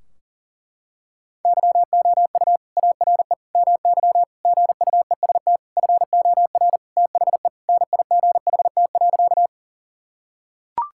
Morse code, heard as a keyed tone, about 40 wpm, 700 Hz. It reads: YOU ARE MY GUEST FOR THE NIGHT.
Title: Sentences from Top 1600 Words 40wpm